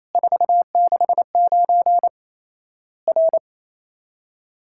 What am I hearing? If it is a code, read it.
Morse code, heard as a keyed tone, about 27 wpm, 700 Hz. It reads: 469 R
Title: Worked All Europe DX Contest - QTC Exchange - 28wpm